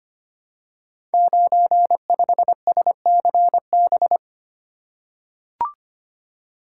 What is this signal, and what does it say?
Morse code, heard as a keyed tone, about 25 wpm, 700 Hz. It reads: 95SCB
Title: Code Groups - 5 Letter:Numbers - 25wpm